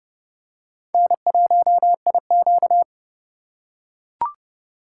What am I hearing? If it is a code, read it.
Morse code, heard as a keyed tone, about 30 wpm, 700 Hz. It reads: N1IQ